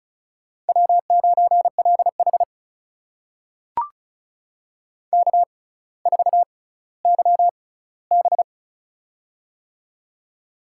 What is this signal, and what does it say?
Morse code, heard as a keyed tone, about 36 wpm, spaced as for 18 wpm, 700 Hz. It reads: W9LH K4YB
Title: USA Call Signs - 4 Characters - 6x Character Spacing - 35wpm